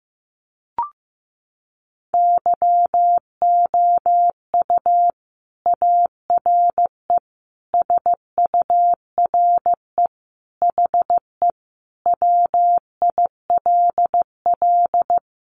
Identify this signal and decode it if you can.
Morse code, heard as a keyed tone, about 15 wpm, 700 Hz. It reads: YOU ARE SURE HE WILL
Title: Sentences from Top 300 Words 15wpm